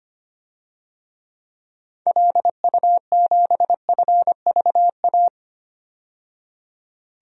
Morse code, heard as a keyed tone, about 25 wpm, 690 Hz. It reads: LU7FVA
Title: DXCC-100 Call Signs - 6 Characters - 25wpm